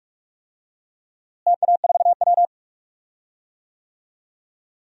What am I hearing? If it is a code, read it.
Morse code, heard as a keyed tone, about 42 wpm, 700 Hz. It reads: TA4W
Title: DXCC-100 Call Signs - 4 Characters - Repeated 3x - 45wpm